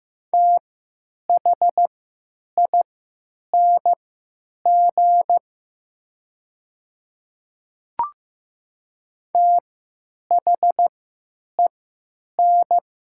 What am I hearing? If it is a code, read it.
Morse code, heard as a keyed tone, about 15 wpm, spaced as for 9 wpm, 700 Hz. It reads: THING THEN
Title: 3x Character Spacing - Top 300 Words - 15wpm